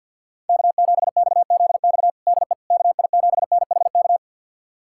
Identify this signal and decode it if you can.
Morse code, heard as a keyed tone, about 52 wpm, 700 Hz. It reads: K7XZX DE KI7NHK
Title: USA Callsign DE Callsign 50wpm